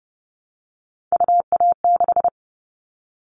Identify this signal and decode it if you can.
Morse code, heard as a keyed tone, about 30 wpm, 700 Hz. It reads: UA6
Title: Callsign Prefixes for Top 25 DXCC - Repeated 3x - 30wpm